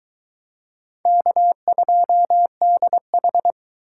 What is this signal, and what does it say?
Morse code, heard as a keyed tone, about 22 wpm, 700 Hz. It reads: K2DH